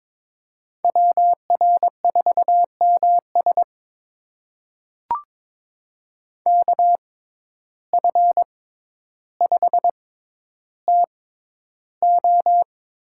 Morse code, heard as a keyed tone, about 22 wpm, spaced as for 11 wpm, 700 Hz. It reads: WR4MS KF5TO
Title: USA Call Signs - 5 Characters - 6x Character Spacing - 22wpm